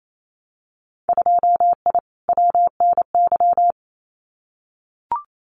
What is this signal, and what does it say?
Morse code, heard as a keyed tone, about 28 wpm, 700 Hz. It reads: 2I WNY
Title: Winter Field Day Exchanges Encore 28wpm